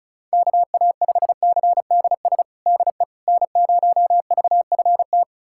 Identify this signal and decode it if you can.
Morse code, heard as a keyed tone, about 34 wpm, 700 Hz. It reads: KA5CDS DE N0VFT
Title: USA Callsign DE Callsign 35wpm